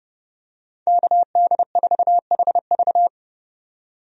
Morse code, heard as a keyed tone, about 31 wpm, 700 Hz. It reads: KD4HV
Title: USA Call Signs - 5 Characters - 30wpm